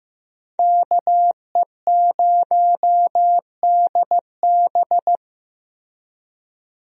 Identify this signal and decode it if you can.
Morse code, heard as a keyed tone, about 15 wpm, 700 Hz. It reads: KE0DB
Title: USA Call Signs - 5 Characters - Repeated 3x - 15wpm